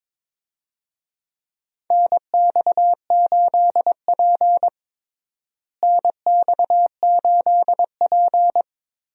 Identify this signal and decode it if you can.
Morse code, heard as a keyed tone, about 22 wpm, 700 Hz. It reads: NX8P NX8P
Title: USA Call Signs - 4 Characters - Repeated 2x - 22wpm